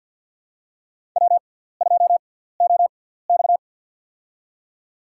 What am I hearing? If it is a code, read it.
Morse code, heard as a keyed tone, about 51 wpm, spaced as for 17 wpm, 700 Hz. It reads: W2OX
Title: USA Call Signs - 4 Characters - 6x Character Spacing - 50wpm